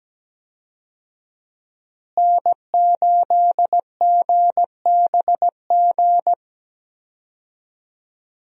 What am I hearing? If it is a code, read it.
Morse code, heard as a keyed tone, about 17 wpm, 700 Hz. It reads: N8GBG
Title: USA Call Signs - 5 Characters - Repeated 3x - 17wpm